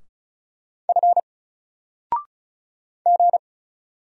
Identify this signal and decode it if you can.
Morse code, heard as a keyed tone, about 33 wpm, 700 Hz. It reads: F G
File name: Single-Letter-Number-35wpm